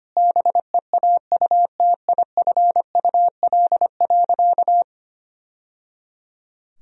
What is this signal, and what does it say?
Morse code, heard as a keyed tone, about 25 wpm, 700 Hz. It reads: BEAUTIFUL.
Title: Sentences from Top 1000 Words - Review Part 3 - 25wpm